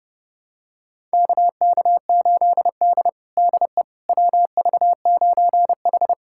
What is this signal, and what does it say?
Morse code, heard as a keyed tone, about 29 wpm, 700 Hz. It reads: KK8D DE WV9H
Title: USA Callsign DE Callsign - 4 Characters - 30wpm